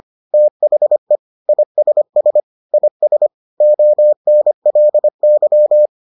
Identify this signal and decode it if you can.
Morse code, heard as a keyed tone, about 25 wpm, 600 Hz. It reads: THE ISS IS ONLY